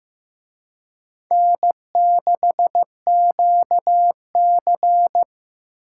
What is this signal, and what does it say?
Morse code, heard as a keyed tone, about 18 wpm, 700 Hz. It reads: N6QC
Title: Speed Racing - USA 1x2 Call Signs 15wpm